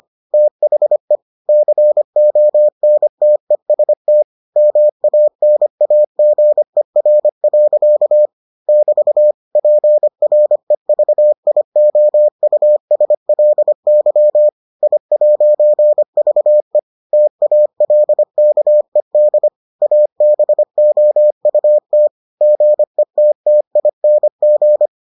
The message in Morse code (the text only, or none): THE CONTEST MANAGER. <BT> PREVIOUSLY I'VE TALKED ABOUT GETTING